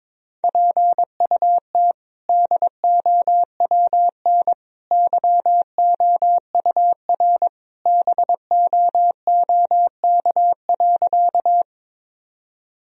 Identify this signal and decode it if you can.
Morse code, heard as a keyed tone, about 22 wpm, 700 Hz. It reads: PUT DOWN YOUR BOOK.
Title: Sentences from Top 1000 Words - Review Part 1 - 22wpm